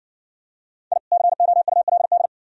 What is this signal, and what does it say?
Morse code, heard as a keyed tone, about 52 wpm, 700 Hz. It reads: I COULD